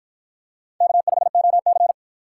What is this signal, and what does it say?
Morse code, heard as a keyed tone, about 53 wpm, 700 Hz. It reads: K5OC